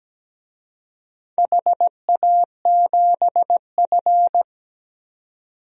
Morse code, heard as a keyed tone, about 17 wpm, 700 Hz. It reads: HA7F
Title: DXCC-100 Call Signs - 4 Characters - 17wpm